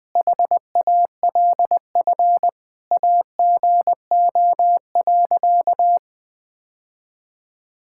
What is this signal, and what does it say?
Morse code, heard as a keyed tone, about 20 wpm, 700 Hz. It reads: HALF AGO.
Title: Sentences with 7-Word Phrases - 20wpm